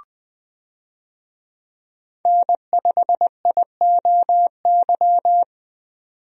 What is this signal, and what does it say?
Morse code, heard as a keyed tone, about 20 wpm, 700 Hz. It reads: N5IOY
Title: USA Call Signs - 5 Characters - 20wpm